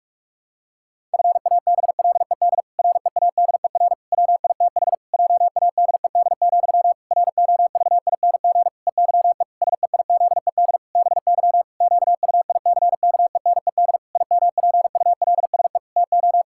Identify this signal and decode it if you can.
Morse code, heard as a keyed tone, about 46 wpm, 700 Hz. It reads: FABLED READER WITH JADED, ROVING EYE SEIZED BY QUICKENED IMPULSE TO
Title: Pangram Challenge v2 - 45wpm